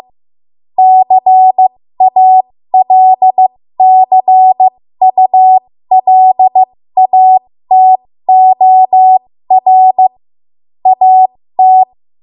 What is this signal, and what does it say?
Morse code, heard as a keyed tone, about 15 wpm, spaced as for 13 wpm, 750 Hz. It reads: CALCULATOR AT
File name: Morse / 13WPM